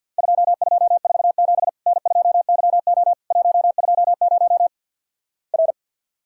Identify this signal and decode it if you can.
Morse code, heard as a keyed tone, about 50 wpm, 700 Hz. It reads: 2247 N2YO 120 R
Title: Worked All Europe DX Contest - QTC Exchange with Repeats - 50wpm